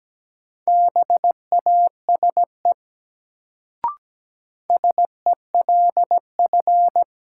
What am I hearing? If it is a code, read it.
Morse code, heard as a keyed tone, about 17 wpm, 700 Hz. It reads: BASE SELF
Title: Sentences from Top 1000 Words - Review Part 1 - New Word Drill - 17wpm